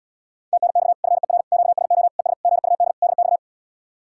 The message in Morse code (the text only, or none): TNX FER QSO ES GUD DX